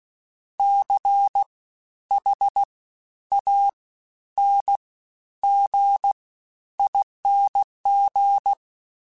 Morse code, heard as a keyed tone, about 16 wpm, 800 Hz. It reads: C H A N G ING
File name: Special INGs-mp3s / Special INGs-mp3s / Fast ing @15